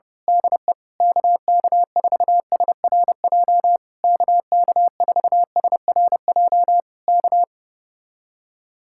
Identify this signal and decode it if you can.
Morse code, heard as a keyed tone, about 29 wpm, 700 Hz. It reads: DE KK4SRJ KK4SRJ K
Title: Speed-Racing Common QSO Phrases - 25wpm